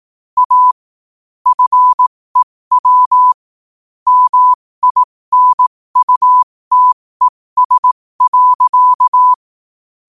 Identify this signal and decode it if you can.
Morse code, heard as a keyed tone, about 18 wpm, spaced as for 15 wpm, 1000 Hz. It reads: A FEW MINUTES.
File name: qotdfiles / 15wpm